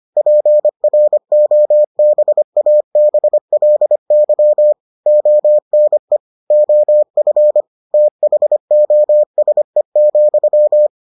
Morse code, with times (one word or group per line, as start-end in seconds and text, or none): 0.2-4.7 PROBABLY
5.1-6.2 ONE
6.5-7.6 OF
7.9-10.9 THOSE,